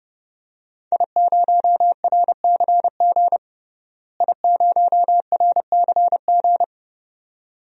Morse code, heard as a keyed tone, about 30 wpm, 700 Hz. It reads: I0RCG I0RCG
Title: DXCC-100 Call Signs - 5 Characters - Repeated 2x - 30wpm